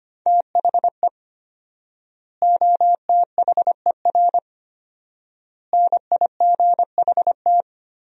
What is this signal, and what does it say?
Morse code, heard as a keyed tone, about 25 wpm, 700 Hz. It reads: THE OTHER NIGHT
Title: 4x Word-Spacing - Sentences from Top 300 Words - 25wpm